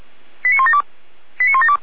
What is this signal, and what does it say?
Five-tone selective call (ZVEI), heard as quick runs of five tones, tones of about 70 ms. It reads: 89172 89172